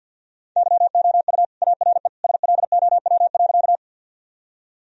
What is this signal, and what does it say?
Morse code, heard as a keyed tone, about 52 wpm, 700 Hz. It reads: YOU ARE SLOW.
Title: Sentences from Top 1000 Words - Review Part 2 - 50wpm